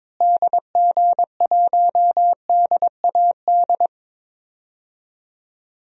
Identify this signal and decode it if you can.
Morse code, heard as a keyed tone, about 22 wpm, 700 Hz. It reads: DG1DAD
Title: DXCC-100 Call Signs - 6 Characters - Repeated 2x - 22wpm